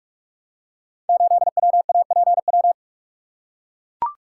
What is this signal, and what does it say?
Morse code, heard as a keyed tone, about 44 wpm, 700 Hz. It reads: 8WAPW
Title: Code Groups - 5 Letter:Numbers - 45wpm